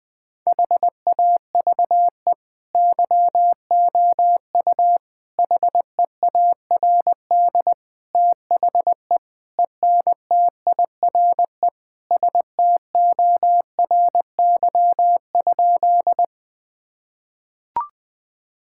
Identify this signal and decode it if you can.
Morse code, heard as a keyed tone, about 21 wpm, 700 Hz. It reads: HAVE YOU HEARD THE ENTIRE STORY?
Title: Sentences from Top 1000 Words - Review Part 5 - 20wpm